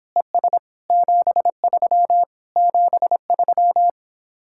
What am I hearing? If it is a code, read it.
Morse code, heard as a keyed tone, about 26 wpm, 700 Hz. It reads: ES 73 73